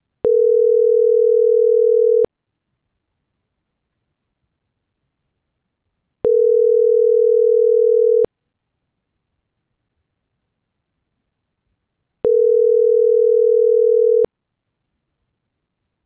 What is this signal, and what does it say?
Ringback tone (North America), 440 + 480 Hz.